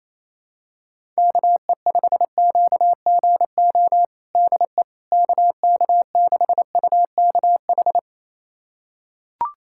Morse code, heard as a keyed tone, about 27 wpm, 700 Hz. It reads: KE5QGO DE KK6UKH
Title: USA Callsign DE Callsign - 6 Characters - 28wpm